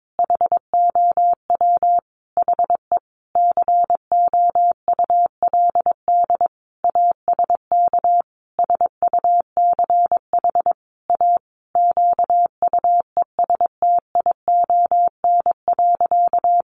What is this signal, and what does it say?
Morse code, heard as a keyed tone, about 22 wpm, 700 Hz. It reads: HOW HE COULD ASK SUCH A QUESTION.
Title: Sentences from Top 600 Words 22wpm